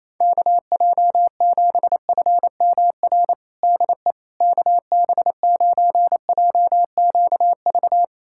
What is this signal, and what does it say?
Morse code, heard as a keyed tone, about 27 wpm, 700 Hz. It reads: KJ7FMR DE KB9JQV
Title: USA Callsign DE Callsign 28wpm